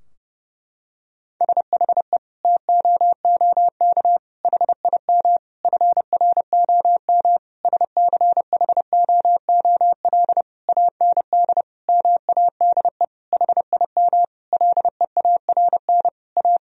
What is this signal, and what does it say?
Morse code, heard as a keyed tone, about 32 wpm, 700 Hz. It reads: SHE TOOK HIM FROM SCHOOL AND MADE HIM LEARN A
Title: Sentences from Top 1000 Words - Review Part 4 - 30wpm